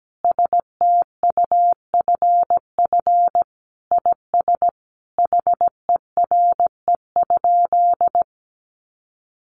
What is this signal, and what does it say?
Morse code, heard as a keyed tone, about 17 wpm, 700 Hz. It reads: STUFF IS HERE?
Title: Sentences from Top 1400 Words 17wpm